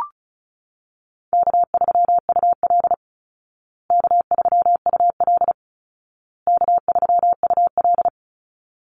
Morse code, heard as a keyed tone, about 34 wpm, 700 Hz. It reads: K3UL K3UL K3UL
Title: Call Signs Repeated 2x - 35wpm